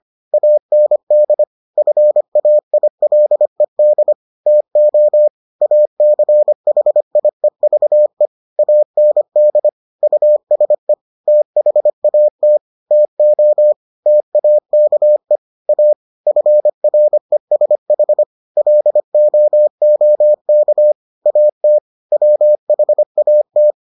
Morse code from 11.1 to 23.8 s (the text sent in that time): THAT TO TAKE A FRESH LOOK AT WHAT